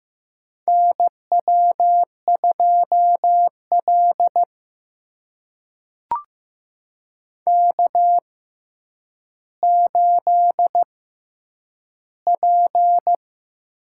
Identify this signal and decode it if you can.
Morse code, heard as a keyed tone, about 15 wpm, spaced as for 9 wpm, 700 Hz. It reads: NW2L K8P
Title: USA Call Signs - 4 Characters - 6x Character Spacing - 15wpm